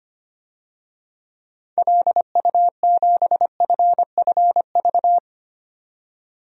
Morse code, heard as a keyed tone, about 25 wpm, 700 Hz. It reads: LU7FFV
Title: DXCC-100 Call Signs - 6 Characters - 25wpm